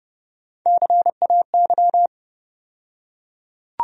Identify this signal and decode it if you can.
Morse code, heard as a keyed tone, about 31 wpm, 700 Hz. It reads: CAY
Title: USA Call Sign Postfixes - 3 Letters 30wpm